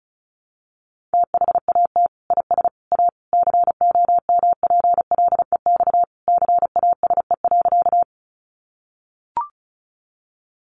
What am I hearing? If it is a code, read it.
Morse code, heard as a keyed tone, about 34 wpm, 700 Hz. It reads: THAT IS A COMPLEX CASE.
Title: Sentences from Top 900 Words 35wpm